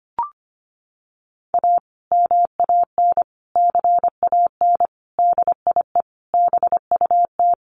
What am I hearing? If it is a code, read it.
Morse code, heard as a keyed tone, about 25 wpm, 700 Hz. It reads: A MAN CAN DIE BUT
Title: Sentences from Top 1300 Words 25wpm